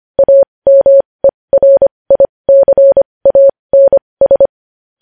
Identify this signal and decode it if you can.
Morse code, heard as a keyed tone, about 25 wpm, spaced as for 20 wpm, 550 Hz. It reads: AMERICANS
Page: news-headlines-in-morse-code-at-20-wpm